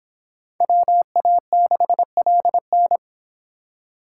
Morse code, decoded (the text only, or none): WA6LN